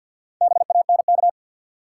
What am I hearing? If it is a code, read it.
Morse code, heard as a keyed tone, about 51 wpm, 700 Hz. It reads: BANK
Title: Sets of 5 Words - New Word Drill - 50wpm